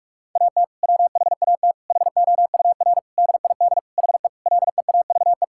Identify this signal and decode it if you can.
Morse code, heard as a keyed tone, about 46 wpm, 700 Hz. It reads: AT WHAT HOUR DID HE LEAVE
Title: Sentences from Top 1600 Words 45wpm